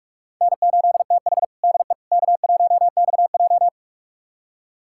Morse code, heard as a keyed tone, about 46 wpm, 700 Hz. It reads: N8TH DE K1XJ
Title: USA Callsign DE Callsign - 4 Characters - 45wpm